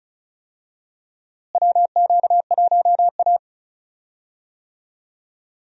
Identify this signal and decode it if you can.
Morse code, heard as a keyed tone, about 36 wpm, 700 Hz. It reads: WQ1A
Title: USA Call Signs - 4 Characters - Repeated 2x - 35wpm